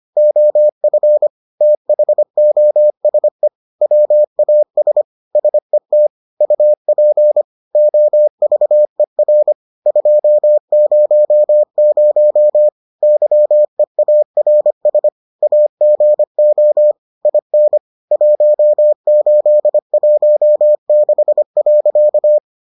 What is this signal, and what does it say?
Morse code, heard as a keyed tone, about 25 wpm, 600 Hz. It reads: OF THOSE WAS SET UP OVER 200 YEARS AGO IN 1816.